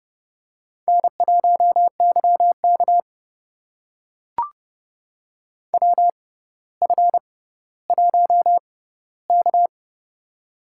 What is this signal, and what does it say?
Morse code, heard as a keyed tone, about 30 wpm, spaced as for 15 wpm, 700 Hz. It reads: N1YK WF1K
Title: USA Call Signs - 4 Characters - 6x Character Spacing - 30wpm